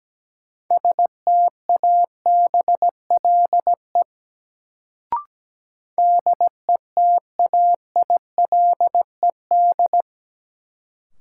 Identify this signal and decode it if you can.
Morse code, heard as a keyed tone, about 17 wpm, 700 Hz. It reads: STABLE DETAILED
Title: Sentences from Top 1500 Words - New Word Drill - 17wpm